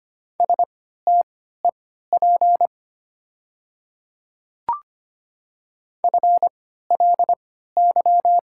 Morse code, heard as a keyed tone, about 25 wpm, spaced as for 14 wpm, 700 Hz. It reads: STEP FLY
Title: 3x Character Spacing - Top 500 Words - 25wpm